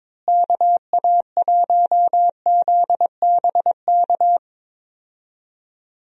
Morse code, heard as a keyed tone, about 22 wpm, 700 Hz. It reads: KA1ZBK
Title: USA Call Signs - 6 Characters - Repeated 3x - 22wpm